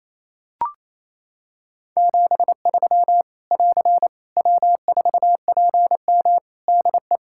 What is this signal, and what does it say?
Morse code, heard as a keyed tone, about 28 wpm, 700 Hz. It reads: 73 <AR> W4PM DE
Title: Practice QSO - Exchange 4 - Ending - 28wpm